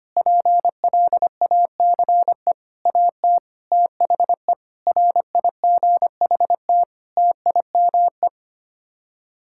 Morse code, heard as a keyed tone, about 25 wpm, 700 Hz. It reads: PLACE AT THE RIGHT TIME
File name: Sets of 9 Words - 25wpm